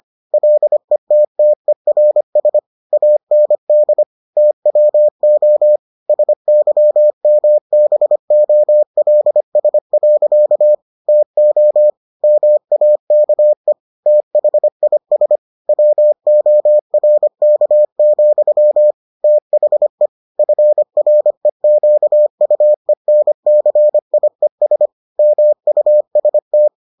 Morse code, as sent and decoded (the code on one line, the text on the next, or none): .-.. - - . .-. ... / .- -. -.. / - .-- --- / ... -.-- -- -... --- .-.. ... .-.-.- / - --- / -- .- -.- . / - .... .. ... / .-- --- .-. -.- --..-- / - .... / ..-. .-. --.- ..- . -. -.-. .. ... / -- ..- ... -
LETTERS AND TWO SYMBOLS. TO MAKE THIS WORK, THE FREQUENCIES MUST